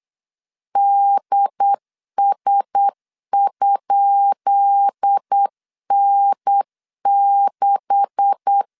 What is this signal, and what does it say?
Morse code, heard as a keyed tone, about 8 wpm, 800 Hz. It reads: DS?N6
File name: LEZIONI / 10 SIGNS / 10B INT 8wpm 3534